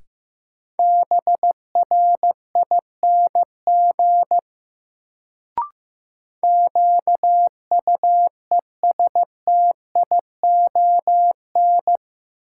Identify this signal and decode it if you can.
Morse code, heard as a keyed tone, about 16 wpm, 700 Hz. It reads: BRING QUESTION
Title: Sets of 5 Words Encore - New Word Drill - 15wpm